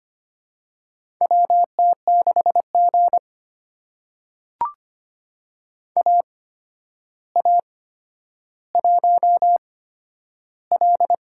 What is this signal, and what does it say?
Morse code, heard as a keyed tone, about 25 wpm, spaced as for 11 wpm, 700 Hz. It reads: WT6G AA1L
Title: USA Call Signs - 8x Character Spacing - 25wpm